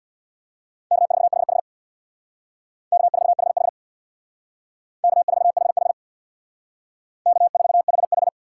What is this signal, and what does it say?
Morse code, heard as a keyed tone, about 62 wpm, 700 Hz. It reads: K4HH K4HH K4HH K4HH
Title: Speed Racing - USA Call Signs 50wpm